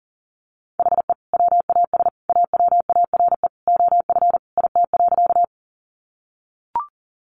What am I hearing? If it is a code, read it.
Morse code, heard as a keyed tone, about 40 wpm, 700 Hz. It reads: HE WAS AWARE OF IT.
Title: Sentences from Top 1200 Words 40wpm